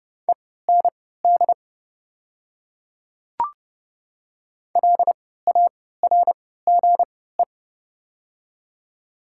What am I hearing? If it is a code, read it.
Morse code, heard as a keyed tone, about 30 wpm, spaced as for 17 wpm, 690 Hz. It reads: END LARGE
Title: 3x Character Spacing - Top 200 Words - 30wpm